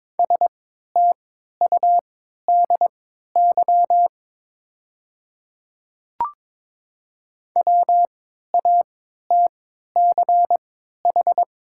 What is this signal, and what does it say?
Morse code, heard as a keyed tone, about 22 wpm, spaced as for 13 wpm, 690 Hz. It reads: STUDY WATCH
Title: 3x Character Spacing - Top 300 Words - 22wpm